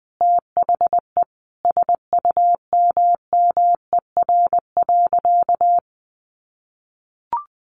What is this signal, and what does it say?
Morse code, heard as a keyed tone, about 20 wpm, 700 Hz. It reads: THE SUMMER.
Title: Sentences from Top 1100 Words 20wpm